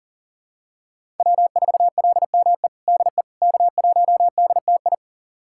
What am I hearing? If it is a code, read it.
Morse code, heard as a keyed tone, about 40 wpm, 690 Hz. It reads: W4LME DE K1DTI
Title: USA Callsign DE Callsign - 5 Characters - 40wpm